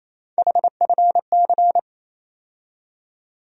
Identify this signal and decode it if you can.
Morse code, heard as a keyed tone, about 28 wpm, 700 Hz. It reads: HFC
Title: USA Call Sign Postfixes - 3 Letters 28wpm